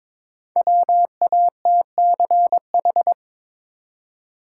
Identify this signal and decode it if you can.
Morse code, heard as a keyed tone, about 22 wpm, 700 Hz. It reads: WATCH
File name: 5 Letter Words - First 3 Letters Given - 22wpm